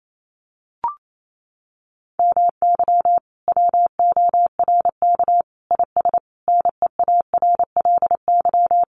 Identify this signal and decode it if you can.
Morse code, heard as a keyed tone, about 28 wpm, 700 Hz. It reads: MY WORK IS NEARLY
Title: Sentences from Top 1300 Words 28wpm